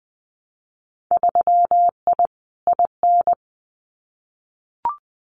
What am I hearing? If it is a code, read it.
Morse code, heard as a keyed tone, about 20 wpm, 700 Hz. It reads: 3I IN
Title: Winter Field Day Exchanges 20wpm